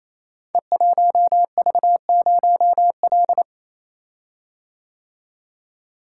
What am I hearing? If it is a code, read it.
Morse code, heard as a keyed tone, about 27 wpm, 700 Hz. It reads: E1V0L
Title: Code Groups - 5 Letter:Numbers - 28wpm